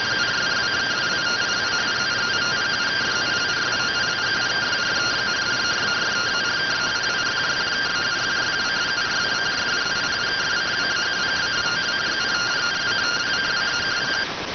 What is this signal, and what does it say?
Two-tone teleprinter signal, about 170 Hz shift.